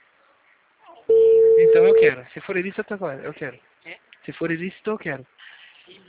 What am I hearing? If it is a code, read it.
Ringback tone (China, 450 Hz), with speech.